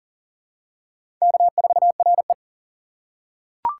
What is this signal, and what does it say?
Morse code, heard as a keyed tone, about 40 wpm, 700 Hz. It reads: K4RE
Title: USA Call Signs - 4 Characters - 2x Character Spacing - 40wpm